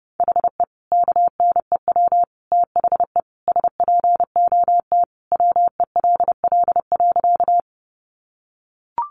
Morse code, heard as a keyed tone, about 30 wpm, 700 Hz. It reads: HE KNEW THE SPOT WELL.
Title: Sentences from Top 1100 Words 30wpm